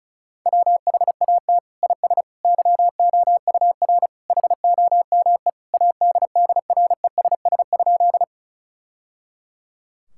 Morse code, heard as a keyed tone, about 36 wpm, 700 Hz. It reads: WHAT IS YOUR HOME ADDRESS?
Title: Sentences from Top 1000 Words - Review Part 6 - 35wpm